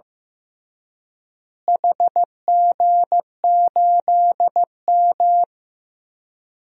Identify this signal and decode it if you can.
Morse code, heard as a keyed tone, about 15 wpm, 700 Hz. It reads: HG8M
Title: DXCC-100 Call Signs - 4 Characters - Repeated 3x - 15wpm